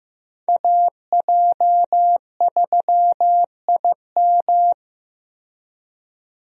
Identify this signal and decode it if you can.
Morse code, heard as a keyed tone, about 15 wpm, 700 Hz. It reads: AJ3IM
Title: USA Call Signs - 5 Characters - 15wpm